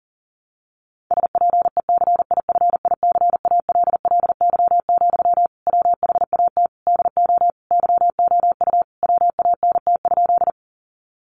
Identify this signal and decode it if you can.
Morse code, heard as a keyed tone, about 40 wpm, 700 Hz. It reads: SPECIFICALLY, WHAT DO YOU WANT?
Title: Sentences from Top 1100 Words 40wpm